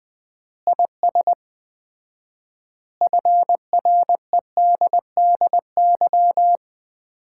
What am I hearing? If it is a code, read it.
Morse code, heard as a keyed tone, about 20 wpm, 700 Hz. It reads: IS FREDDY
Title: Common QSO Phrases - 4x Word-Spacing - 20wpm